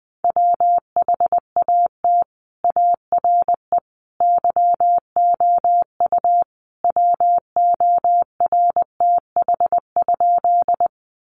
Morse code, heard as a keyed tone, about 20 wpm, 700 Hz. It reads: WHAT ARE YOU WORTH?
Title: Sentences from Top 800 Words 20wpm